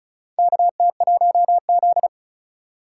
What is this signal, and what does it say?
Morse code, heard as a keyed tone, about 36 wpm, 700 Hz. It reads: KT1Z
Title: USA Call Signs - 4 Characters - 3x Character Spacing - 35wpm